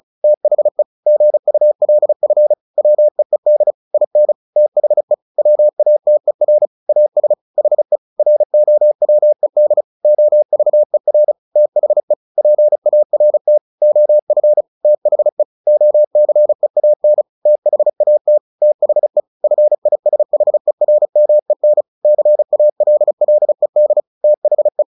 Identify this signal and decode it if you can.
Morse code, heard as a keyed tone, about 34 wpm, 600 Hz. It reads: THE GULF WEED IN THE WATER AS HE ROWED OVER THE PART OF THE OCEAN THAT THE FISHERMEN CALLED THE
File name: the old man and the sea / the old man and the sea 1 35wpm